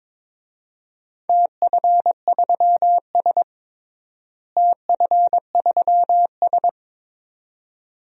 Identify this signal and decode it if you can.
Morse code, heard as a keyed tone, about 22 wpm, 700 Hz. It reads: TF3S TF3S
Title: DXCC-100 Call Signs - 4 Characters - Repeated 2x - 22wpm